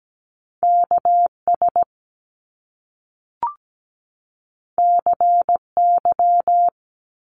Morse code, heard as a keyed tone, about 17 wpm, 700 Hz. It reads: KS CY
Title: Top Two Letter Combinations in Words 17wpm